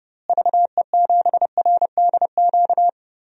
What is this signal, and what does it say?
Morse code, heard as a keyed tone, about 30 wpm, 700 Hz. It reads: VE7RDQ
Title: DXCC-100 Call Signs - 6 Characters - Repeated 3x - 30wpm